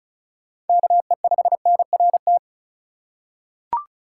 Morse code, heard as a keyed tone, about 36 wpm, 700 Hz. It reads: KE5NRT
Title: USA Call Signs - 6 Characters - 2x Character Spacing - 35wpm